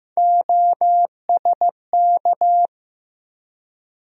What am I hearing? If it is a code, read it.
Morse code, heard as a keyed tone, about 15 wpm, 700 Hz. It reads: OSK
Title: USA Call Sign Postfixes - 3 Letters 15wpm